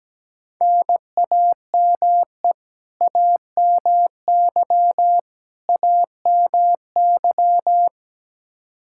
Morse code, heard as a keyed tone, about 17 wpm, 700 Hz. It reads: NAME AMY AMY
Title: Practice QSO - Exchange 1 - Beginning - 17wpm